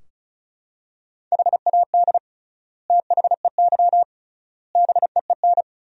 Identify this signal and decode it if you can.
Morse code, heard as a keyed tone, about 36 wpm, 700 Hz. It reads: HAD THEY BEEN
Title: 3x Word-Spacing - Sentences from Top 400 Words - 35wpm